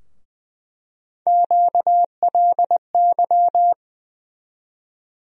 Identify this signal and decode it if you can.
Morse code, heard as a keyed tone, about 19 wpm, 700 Hz. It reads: QLY
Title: USA Call Sign Postfixes - 3 Letters 20wpm